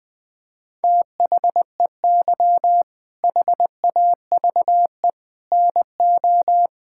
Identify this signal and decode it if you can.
Morse code, heard as a keyed tone, about 21 wpm, 700 Hz. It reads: THEY HAVE NO
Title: All-Age Appropriate Jokes - 20wpm